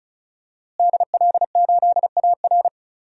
Morse code, heard as a keyed tone, about 34 wpm, 700 Hz. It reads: DL8AR
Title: DXCC-100 Call Signs - 5 Characters - Repeated 3x - 35wpm